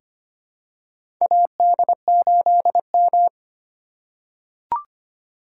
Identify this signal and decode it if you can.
Morse code, heard as a keyed tone, about 25 wpm, 700 Hz. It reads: AD8M
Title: USA Call Signs - 4 Characters - 2x Character Spacing - 25wpm